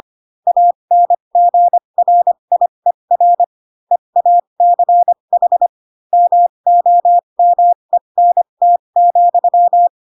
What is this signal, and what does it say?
Morse code, heard as a keyed tone, about 25 wpm, spaced as for 22 wpm, 700 Hz. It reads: ANGRIER EACH MOMENT,
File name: Pinocchio / Pinocchio ch2 18 25 ik1hge